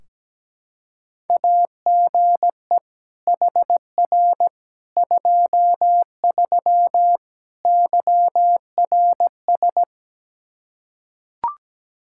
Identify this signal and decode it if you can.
Morse code, heard as a keyed tone, about 17 wpm, 700 Hz. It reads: AGE HR 23 YRS
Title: Practice QSO - Exchange 3 - More Info - 17wpm